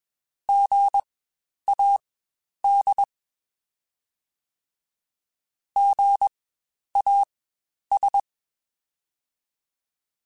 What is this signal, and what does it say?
Morse code, heard as a keyed tone, about 21 wpm, spaced as for 10 wpm, 800 Hz. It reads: GAD GAS
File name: Spaced Alphas / Spaced Alphas mp3s / Gs sp @20